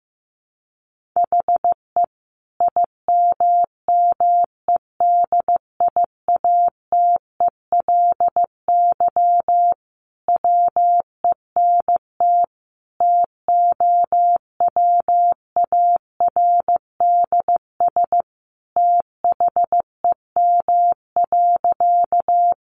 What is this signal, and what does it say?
Morse code, heard as a keyed tone, about 15 wpm, 700 Hz. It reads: HE IMMEDIATELY WENT TOWARDS THEM.
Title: Sentences from Top 1100 Words 15wpm